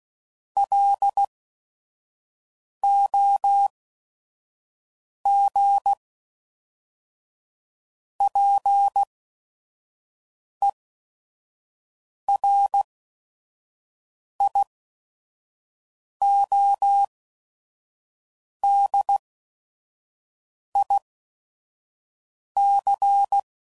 Morse code, heard as a keyed tone, about 16 wpm, spaced as for 5 wpm, 800 Hz. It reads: LOG PERIODIC
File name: CW Code Groups @5WPM / fromSM's / Antennas